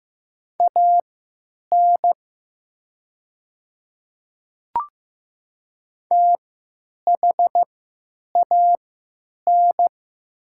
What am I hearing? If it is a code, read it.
Morse code, heard as a keyed tone, about 15 wpm, spaced as for 9 wpm, 700 Hz. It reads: AN THAN
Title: 3x Character Spacing - Top 200 Words - 15wpm